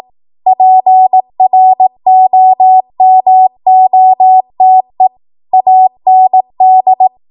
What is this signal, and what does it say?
Morse code, heard as a keyed tone, about 18 wpm, 750 Hz. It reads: PROMOTE AND
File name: Morse / 18WPM